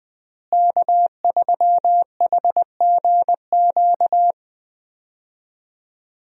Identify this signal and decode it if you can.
Morse code, heard as a keyed tone, about 20 wpm, 700 Hz. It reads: K3HGQ
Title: USA Call Signs - 5 Characters - Repeated 3x - 20wpm